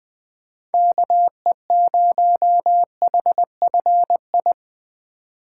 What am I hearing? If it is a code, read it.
Morse code, heard as a keyed tone, about 20 wpm, 700 Hz. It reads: KE0HFI
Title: Speed Racing - USA Call Signs 17wpm